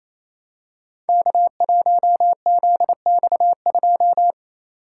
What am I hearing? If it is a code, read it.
Morse code, heard as a keyed tone, about 29 wpm, 700 Hz. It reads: K1ZX2
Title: Code Groups - 5 Letter:Numbers - 28wpm